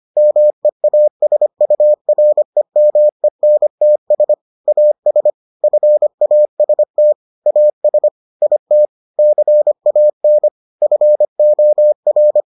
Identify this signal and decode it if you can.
Morse code, heard as a keyed tone, about 25 wpm, 600 Hz. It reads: MEASUREMENTS AS FAST AS IT CAN FOR